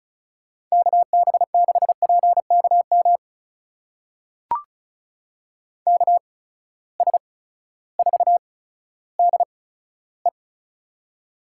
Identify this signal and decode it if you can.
Morse code, heard as a keyed tone, about 36 wpm, spaced as for 16 wpm, 700 Hz. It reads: KB6PKM KS4DE
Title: USA Call Signs - 8x Character Spacing - 35wpm